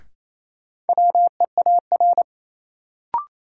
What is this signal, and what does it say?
Morse code, heard as a keyed tone, about 28 wpm, 700 Hz. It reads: WEAR
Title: Sentences from Top 1000 Words - Review Part 6 - New Word Drill - 28wpm